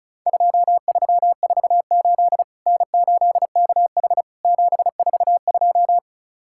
Morse code, heard as a keyed tone, about 36 wpm, 700 Hz. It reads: 2348 N8KH 742
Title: Worked All Europe DX Contest - QTC Exchange - 35wpm